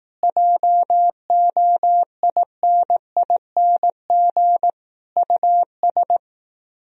Morse code, heard as a keyed tone, about 18 wpm, 700 Hz. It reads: JOINING US
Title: Speed-Racing - Sets of 5 Words 15wpm